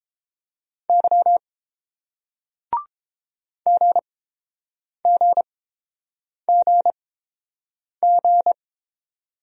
Morse code, heard as a keyed tone, about 30 wpm, spaced as for 7 wpm, 690 Hz. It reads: Y GGGG